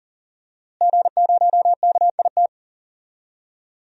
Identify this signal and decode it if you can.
Morse code, heard as a keyed tone, about 38 wpm, 700 Hz. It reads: G0KIT
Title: DXCC-100 Call Signs - 5 Characters - Repeated 2x - 40wpm